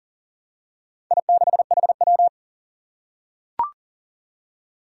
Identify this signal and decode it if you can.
Morse code, heard as a keyed tone, about 40 wpm, 700 Hz. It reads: I6HW
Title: Four Challenging Letter-Number Combinations 40wpm